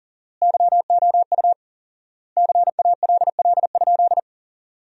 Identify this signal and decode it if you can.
Morse code, heard as a keyed tone, about 40 wpm, 700 Hz. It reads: YOU CALL?
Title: 4x Word-Spacing - Sentences from Top 100 Words - 40wpm